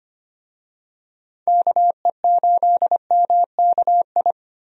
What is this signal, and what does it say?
Morse code, heard as a keyed tone, about 25 wpm, 700 Hz. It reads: KE8MKI
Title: USA Call Signs - 6 Characters - Repeated 3x - 25wpm